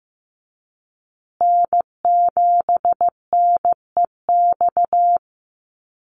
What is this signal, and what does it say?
Morse code, heard as a keyed tone, about 15 wpm, 700 Hz. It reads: N7NEX
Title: USA Call Signs Repeated 2x 15wpm